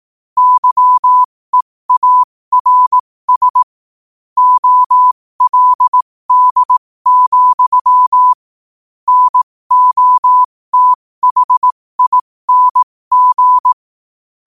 Morse code, heard as a keyed tone, about 18 wpm, spaced as for 15 wpm, 1000 Hz. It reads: YEARS OLD, NOTHING
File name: qotdfiles / 15wpm